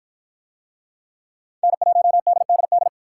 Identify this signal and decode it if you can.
Morse code, heard as a keyed tone, about 51 wpm, 700 Hz. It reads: N1DDD